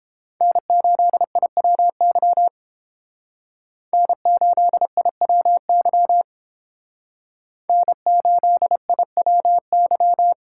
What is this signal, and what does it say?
Morse code, heard as a keyed tone, about 29 wpm, 700 Hz. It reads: N8IWY N8IWY N8IWY